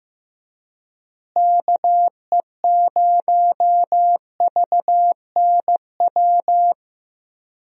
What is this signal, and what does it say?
Morse code, heard as a keyed tone, about 16 wpm, 700 Hz. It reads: KE0VNW
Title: USA Call Signs - 6 Characters - Repeated 3x - 15wpm